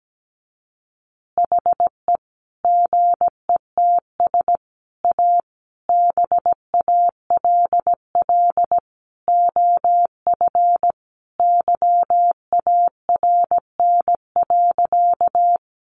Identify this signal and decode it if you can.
Morse code, heard as a keyed tone, about 17 wpm, 700 Hz. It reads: HE GETS A BALL OF YARN.
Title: Additional Easy Phrases 17wpm